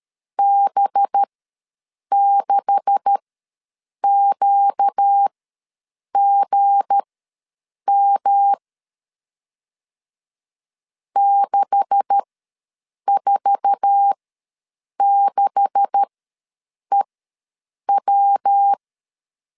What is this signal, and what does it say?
Morse code, heard as a keyed tone, about 13 wpm, spaced as for 7 wpm, 800 Hz. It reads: B6QGM 646EW
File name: LEZIONI / 9 286 ws 12wpm 3351